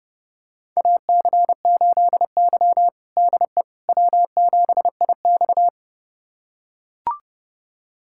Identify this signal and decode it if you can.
Morse code, heard as a keyed tone, about 32 wpm, 700 Hz. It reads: AC8Y DE W7IX
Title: USA Callsign DE Callsign - 4 Characters - 30wpm